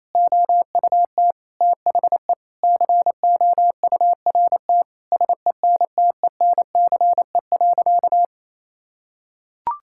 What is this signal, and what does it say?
Morse code, heard as a keyed tone, about 28 wpm, 700 Hz. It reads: OUT THE COURT SENTENCE.
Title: Sentences from Top 1600 Words 28wpm